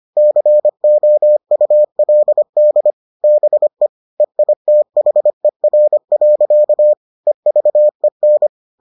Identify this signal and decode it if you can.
Morse code, heard as a keyed tone, about 25 wpm, 600 Hz. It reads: COULD BE EITHER. EVEN